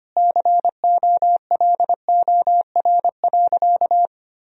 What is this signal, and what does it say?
Morse code, heard as a keyed tone, about 25 wpm, 700 Hz. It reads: COLOR.
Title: Sentences from Top 1000 Words - Review Part 2 - 25wpm